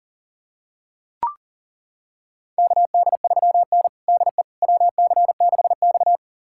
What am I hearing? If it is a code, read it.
Morse code, heard as a keyed tone, about 37 wpm, 700 Hz. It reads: KD3N DE WC6X
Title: USA Callsign DE Callsign - 4 Characters - 40wpm